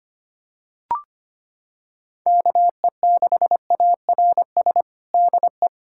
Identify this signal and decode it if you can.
Morse code, heard as a keyed tone, about 25 wpm, 700 Hz. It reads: KE6ARS DE
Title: USA Callsign DE Callsign - 6 Characters - 25wpm